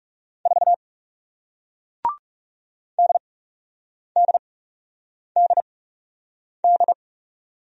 Morse code, heard as a keyed tone, about 41 wpm, spaced as for 8 wpm, 700 Hz. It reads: 4 DDDD